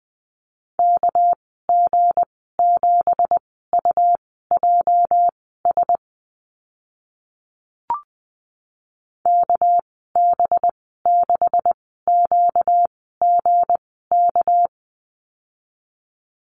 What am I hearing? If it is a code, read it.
Morse code, heard as a keyed tone, about 20 wpm, spaced as for 15 wpm, 700 Hz. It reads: KG7UJS KB6QGK
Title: 245 Lesson 39g - Letter z - Call Signs 20wpm